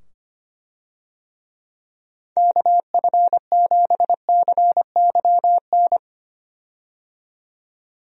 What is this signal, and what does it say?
Morse code, heard as a keyed tone, about 25 wpm, 700 Hz. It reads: KF7CYN
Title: USA Call Signs - 6 Characters - 25wpm